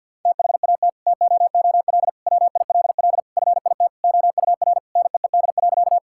Morse code, heard as a keyed tone, about 48 wpm, 700 Hz. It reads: THAT TOOL WILL FIT OUR NEED.
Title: Sentences from Top 1000 Words - Review Part 6 - 50wpm